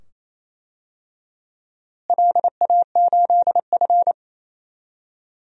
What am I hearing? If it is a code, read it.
Morse code, heard as a keyed tone, about 28 wpm, 700 Hz. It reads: LA8F
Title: DXCC-100 Call Signs - 4 Characters - Repeated 2x - 28wpm